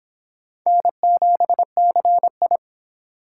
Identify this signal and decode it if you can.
Morse code, heard as a keyed tone, about 25 wpm, 700 Hz. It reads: N7CI